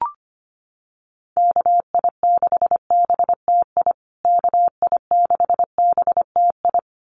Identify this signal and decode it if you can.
Morse code, heard as a keyed tone, about 25 wpm, 700 Hz. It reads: KI6BTI KI6BTI
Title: USA Call Signs Repeated 2x 25wpm